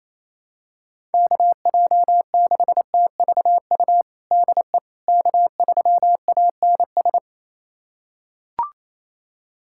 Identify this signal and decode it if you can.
Morse code, heard as a keyed tone, about 28 wpm, 700 Hz. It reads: KJ6TVU DE K3ANS
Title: USA Callsign DE Callsign 28wpm